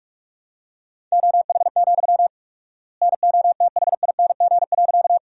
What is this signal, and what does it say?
Morse code, heard as a keyed tone, about 41 wpm, 700 Hz. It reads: OH, NOTHING.
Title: 4x Word-Spacing - Sentences from Top 500 Words - 45wpm